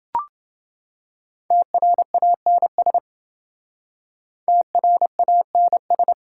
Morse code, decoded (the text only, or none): TRANS TRANS